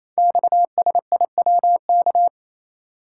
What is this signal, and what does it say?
Morse code, heard as a keyed tone, about 29 wpm, 700 Hz. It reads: XSIWK
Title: Code Groups - 5 Letters - 28wpm